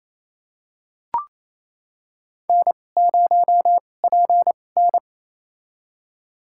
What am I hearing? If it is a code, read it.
Morse code, heard as a keyed tone, about 28 wpm, spaced as for 20 wpm, 700 Hz. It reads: N0PN
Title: USA Call Signs - 4 Characters - 2x Character Spacing - 28wpm